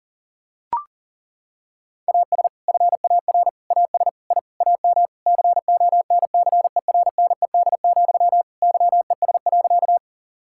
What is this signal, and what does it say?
Morse code, heard as a keyed tone, about 40 wpm, 690 Hz. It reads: AS FAR AS I AM CONCERNED, YES.
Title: Sentences from Top 1000 Words - Review Part 6 - 40wpm